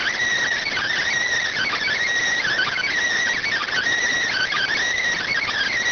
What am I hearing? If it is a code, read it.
Emergency Alert System (SAME) tones, heard as harsh buzzing bursts. It is heard from 0.7 to 4.7 s.